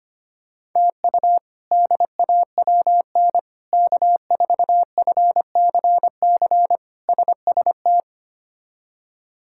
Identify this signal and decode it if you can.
Morse code, heard as a keyed tone, about 25 wpm, 700 Hz. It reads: TU DAWN K4FCC SST
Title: K1USN SST QSO Practice - 25wpm